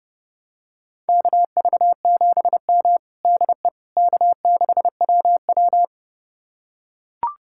Morse code, heard as a keyed tone, about 32 wpm, 700 Hz. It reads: KV7M DE K6WW
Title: USA Callsign DE Callsign - 4 Characters - 30wpm